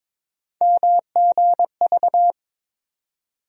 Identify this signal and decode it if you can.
Morse code, heard as a keyed tone, about 23 wpm, 700 Hz. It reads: MGV
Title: USA Call Sign Postfixes - 3 Letters 22wpm